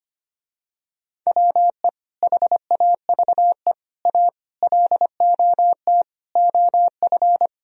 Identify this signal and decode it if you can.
Morse code, heard as a keyed tone, about 25 wpm, 700 Hz. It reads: WE HAVE A LOT OF
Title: Speed-Racing - Sets of 5 Words 25wpm